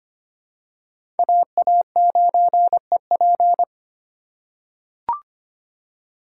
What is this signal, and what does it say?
Morse code, heard as a keyed tone, about 25 wpm, 700 Hz. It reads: AA9EP
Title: USA Call Signs - 5 Characters - 8x Character Spacing - 25wpm